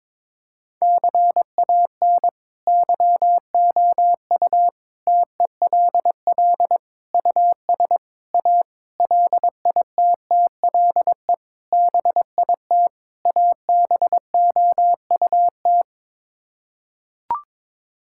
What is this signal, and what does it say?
Morse code, heard as a keyed tone, about 22 wpm, 700 Hz. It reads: CAN YOU TELL US A LITTLE BIT ABOUT
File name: practice / 22wpm / Sets of 8 Words - 22wpm